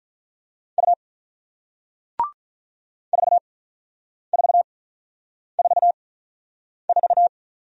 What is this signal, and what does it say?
Morse code, heard as a keyed tone, about 47 wpm, spaced as for 9 wpm, 700 Hz. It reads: U 4444